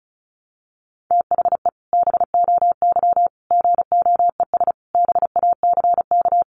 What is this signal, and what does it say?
Morse code, heard as a keyed tone, about 34 wpm, 700 Hz. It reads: THE BOY GOES BACK
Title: Additional Easy Phrases 35wpm